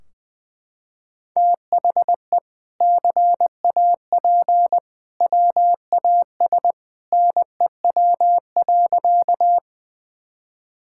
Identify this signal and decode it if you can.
Morse code, heard as a keyed tone, about 21 wpm, 700 Hz. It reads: THE CAP WAS NEW.
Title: Sentences from Top 1600 Words 20wpm